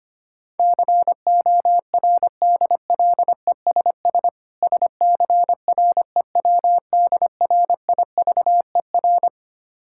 Morse code, heard as a keyed tone, about 25 wpm, 690 Hz. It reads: CORDLESS SCREWDRIVER
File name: practice / 25wpm / Difficult Words with Hints - 25wpm